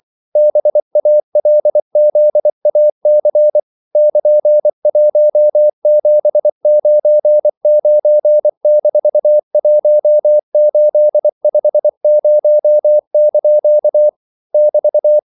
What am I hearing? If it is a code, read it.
Morse code, heard as a keyed tone, about 24 wpm, 600 Hz. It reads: BALZAC <KN>1799-1850) <BT>
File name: CW / Practice / EnglishQuotations0324